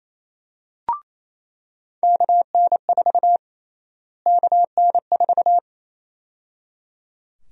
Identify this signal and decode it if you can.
Morse code, heard as a keyed tone, about 27 wpm, 700 Hz. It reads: KN4 KN4
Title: USA Call Sign Prefixes - Repeated 2x - 28wpm